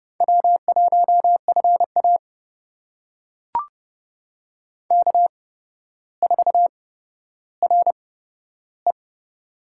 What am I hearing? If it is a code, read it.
Morse code, heard as a keyed tone, about 30 wpm, spaced as for 13 wpm, 700 Hz. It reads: W1FA K4RE
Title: USA Call Signs - 4 Characters - 8x Character Spacing - 30wpm